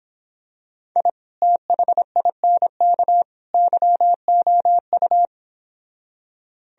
Morse code, heard as a keyed tone, about 26 wpm, 700 Hz. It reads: I THINK YOU